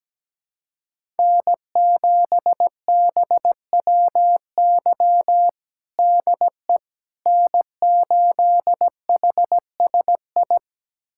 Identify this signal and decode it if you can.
Morse code, heard as a keyed tone, about 17 wpm, 700 Hz. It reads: N7BWY DE N8HSI
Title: USA Callsign DE Callsign - 5 Characters - 17wpm